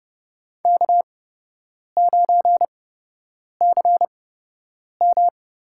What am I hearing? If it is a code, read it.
Morse code, heard as a keyed tone, about 30 wpm, spaced as for 8 wpm, 700 Hz. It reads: K9CM
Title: USA Call Signs - 8x Character Spacing - 30wpm